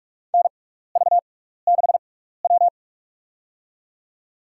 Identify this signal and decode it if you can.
Morse code, heard as a keyed tone, about 41 wpm, spaced as for 15 wpm, 700 Hz. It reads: NV6W
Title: USA Call Signs - 4 Characters - 6x Character Spacing - 45wpm